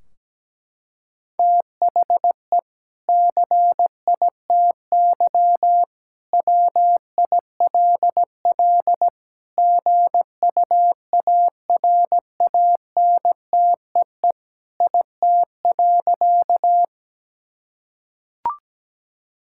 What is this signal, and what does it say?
Morse code, heard as a keyed tone, about 17 wpm, 700 Hz. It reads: THE CITY WILL GUARANTEE IT.
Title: Sentences from Top 1600 Words 17wpm